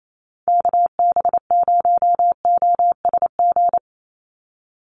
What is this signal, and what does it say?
Morse code, heard as a keyed tone, about 28 wpm, 700 Hz. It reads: KB0OSG
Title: USA Call Signs 28wpm